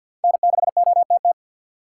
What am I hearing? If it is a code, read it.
Morse code, heard as a keyed tone, about 52 wpm, 700 Hz. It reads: N6OTT
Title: USA Call Signs - 4x Character Spacing - 50wpm